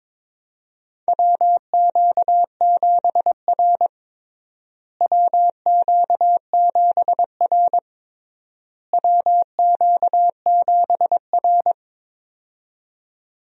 Morse code, heard as a keyed tone, about 22 wpm, 700 Hz. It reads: WQ7R WQ7R WQ7R
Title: USA Call Signs - 4 Characters - Repeated 3x - 22wpm